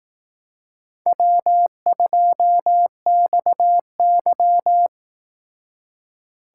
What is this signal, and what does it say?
Morse code, heard as a keyed tone, about 18 wpm, 700 Hz. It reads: W2XY
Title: Speed Racing - USA 1x2 Call Signs 15wpm